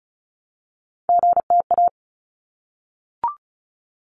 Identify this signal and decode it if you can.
Morse code, heard as a keyed tone, about 34 wpm, 700 Hz. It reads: GTA
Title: ARRL Sections 35wpm